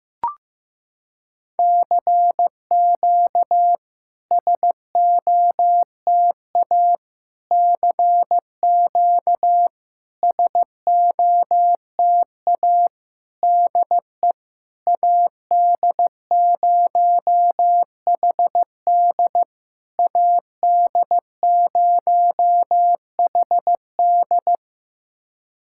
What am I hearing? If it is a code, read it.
Morse code, heard as a keyed tone, about 16 wpm, 700 Hz. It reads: CQ SOTA CQ SOTA DE AD0HD AD0HD
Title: SOTA QSO Practice 15wpm